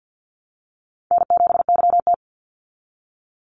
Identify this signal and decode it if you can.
Morse code, heard as a keyed tone, about 50 wpm, 700 Hz. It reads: N7YT
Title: Easy Call Signs Repeated 3x - 50wpm